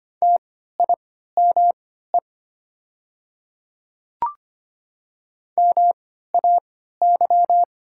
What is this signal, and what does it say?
Morse code, heard as a keyed tone, about 25 wpm, spaced as for 14 wpm, 700 Hz. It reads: TIME MAY
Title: 3x Character Spacing - Top 100 Words - 25wpm